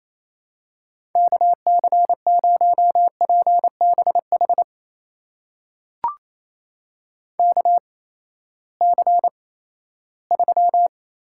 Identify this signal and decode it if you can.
Morse code, heard as a keyed tone, about 28 wpm, spaced as for 16 wpm, 700 Hz. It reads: KC0PBH KC3
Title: USA Call Signs - 6 Characters - 8x Character Spacing - 28wpm